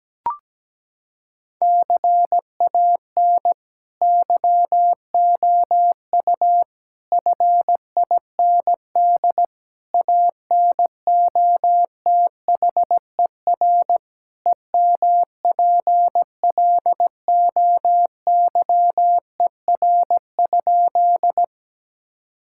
Morse code, heard as a keyed tone, about 17 wpm, 700 Hz. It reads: CAN YOU FIND ANOTHER EMPLOYER?
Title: Sentences from Top 1600 Words 17wpm